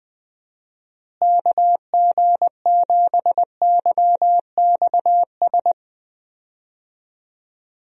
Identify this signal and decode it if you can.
Morse code, heard as a keyed tone, about 20 wpm, 700 Hz. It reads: KG7YXS
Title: USA Call Signs - 6 Characters - Repeated 3x - 20wpm